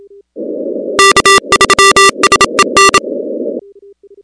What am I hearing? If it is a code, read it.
Morse code, heard as a keyed tone, about 27 wpm, 1180 Hz, with noise. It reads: K3SEN
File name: contests / 2019-cw-wpx-kt5j-k5pi- -2330 0-16